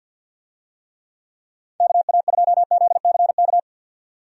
Morse code, heard as a keyed tone, about 48 wpm, 700 Hz. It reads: KA2ZCK
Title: USA Call Signs - 6 Characters - 50wpm